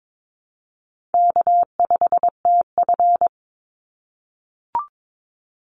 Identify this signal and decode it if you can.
Morse code, heard as a keyed tone, about 22 wpm, 700 Hz. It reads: K5TF
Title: Easy Call Signs - 22wpm